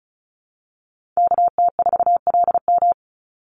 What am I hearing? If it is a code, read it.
Morse code, heard as a keyed tone, about 34 wpm, 700 Hz. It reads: KT4LM
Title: USA Call Signs 35wpm